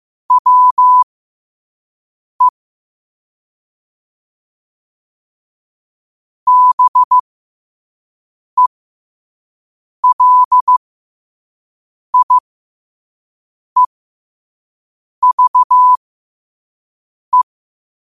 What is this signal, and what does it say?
Morse code, heard as a keyed tone, about 15 wpm, spaced as for 5 wpm, 1000 Hz. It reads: WE BELIEVE
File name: qotdfiles / 5wpm